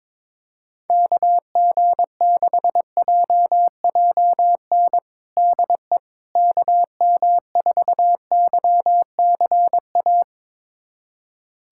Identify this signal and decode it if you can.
Morse code, heard as a keyed tone, about 22 wpm, 700 Hz. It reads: KG6JJN DE KM4YCA
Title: USA Callsign DE Callsign - 6 Characters - 22wpm